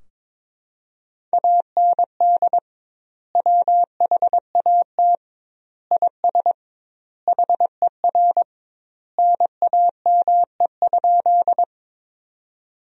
Morse code, heard as a keyed tone, about 22 wpm, 700 Hz. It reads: AND WHAT IS HER NAME?
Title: 2x Word-Spacing - Sentences from Top 200 Words - 22wpm